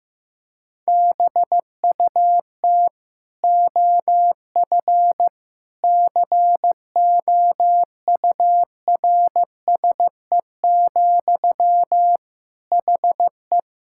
Morse code, heard as a keyed tone, about 15 wpm, 700 Hz. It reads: BUT OF COURSE, HE
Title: Sentences from Top 1600 Words 15wpm